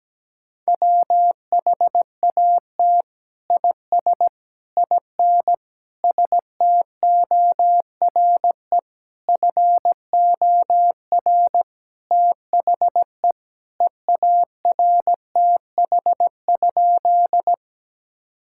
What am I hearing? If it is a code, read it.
Morse code, heard as a keyed tone, about 17 wpm, 700 Hz. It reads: WHAT IS IN STORE FOR THE EARTH?
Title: Sentences from Top 1000 Words - Review Part 5 - 17wpm